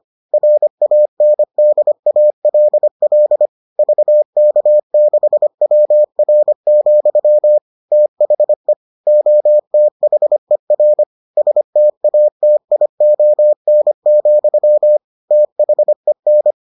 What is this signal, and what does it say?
Morse code, heard as a keyed tone, about 25 wpm, 600 Hz. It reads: RANDALL VK6WR, THE OTHER STATION, THEN